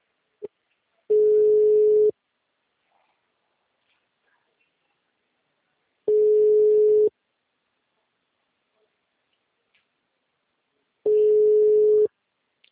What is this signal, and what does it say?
Ringback tone (Europe, 425 Hz).